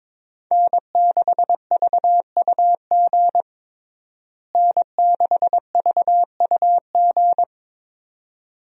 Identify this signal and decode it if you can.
Morse code, heard as a keyed tone, about 22 wpm, 700 Hz. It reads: N6VUG N6VUG
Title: USA Call Signs - 5 Characters - Repeated 2x - 22wpm